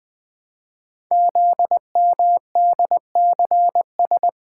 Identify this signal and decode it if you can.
Morse code, heard as a keyed tone, about 20 wpm, 700 Hz. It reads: ZMDCS
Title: Code Groups - 5 Letters - 20wpm